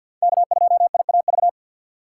Morse code, heard as a keyed tone, about 49 wpm, 700 Hz. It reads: K2IAV
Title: USA Call Signs - 5 Characters - 4x Character Spacing - 50wpm